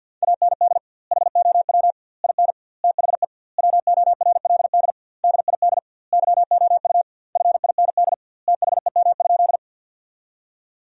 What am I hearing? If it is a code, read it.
Morse code, heard as a keyed tone, about 52 wpm, 700 Hz. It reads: AND HOW IN THE WORLD DID YOU FIND THEM?
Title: 2x Word-Spacing - Sentences from Top 200 Words - 50wpm